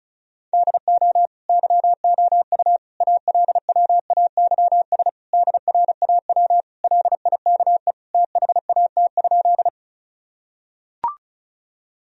Morse code, heard as a keyed tone, about 34 wpm, 700 Hz. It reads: DO YOU ALWAYS DRAW LIKE THAT?
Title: Sentences from Top 1000 Words - Review Part 1 - 35wpm